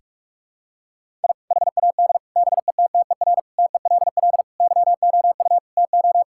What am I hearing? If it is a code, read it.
Morse code, heard as a keyed tone, about 46 wpm, 700 Hz. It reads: I HAD BETTER TELL YOU TO
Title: Sentences from Top 1000 Words - Review Part 6 - 45wpm